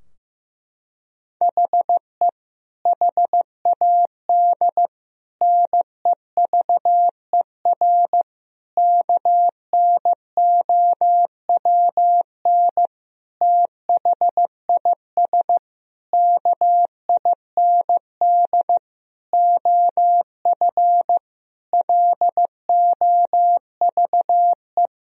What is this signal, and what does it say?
Morse code, heard as a keyed tone, about 15 wpm, 700 Hz. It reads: HE HAD NEVER KNOWN THIS KIND OF LOVE
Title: Sentences from Top 1000 Words - Review Part 3 - 15wpm